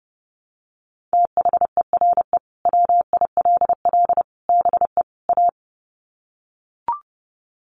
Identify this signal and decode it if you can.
Morse code, heard as a keyed tone, about 30 wpm, 700 Hz. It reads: THERE WILL BE A
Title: Sets of 4 Words 30wpm